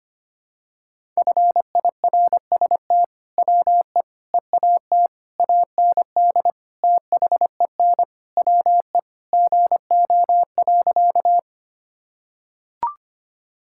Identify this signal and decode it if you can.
Morse code, heard as a keyed tone, about 25 wpm, 700 Hz. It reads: FIRST WE EAT AND THEN WE GO.
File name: practice / 25wpm / Sentences with 2-Word Phrases - 25wpm